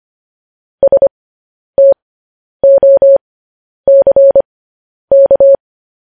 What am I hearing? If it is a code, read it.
Morse code, heard as a keyed tone, about 25 wpm, spaced as for 10 wpm, 550 Hz. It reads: STOCK